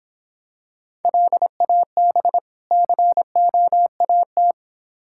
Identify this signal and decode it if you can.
Morse code, heard as a keyed tone, about 25 wpm, 700 Hz. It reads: LAB COAT